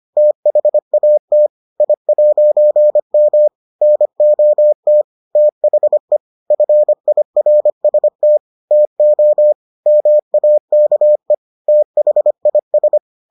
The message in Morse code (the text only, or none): THAT I'M NOT THE FIRST TO MAKE THIS